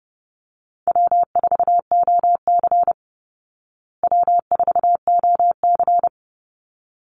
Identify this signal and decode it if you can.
Morse code, heard as a keyed tone, about 30 wpm, 700 Hz. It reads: W4OC W4OC
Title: Easy Call Signs Repeated 2x - 30wpm